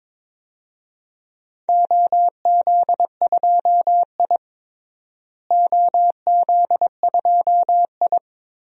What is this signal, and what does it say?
Morse code, heard as a keyed tone, about 22 wpm, 700 Hz. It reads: OZ2I OZ2I
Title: DXCC-100 Call Signs - 4 Characters - Repeated 3x - 22wpm